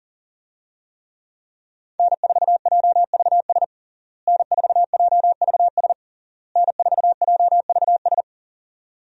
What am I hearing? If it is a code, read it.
Morse code, heard as a keyed tone, about 37 wpm, 700 Hz. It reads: N4JVS N4JVS N4JVS
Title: USA Call Signs - 5 Characters - Repeated 3x - 40wpm